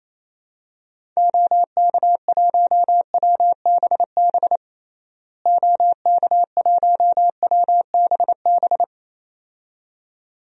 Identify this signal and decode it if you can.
Morse code, heard as a keyed tone, about 28 wpm, 700 Hz. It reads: OK1WBB OK1WBB
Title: DXCC-100 Call Signs - 6 Characters - Repeated 2x - 28wpm